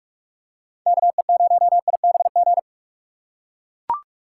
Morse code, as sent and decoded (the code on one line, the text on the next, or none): -.- . ----- .. -... --.
KE0IBG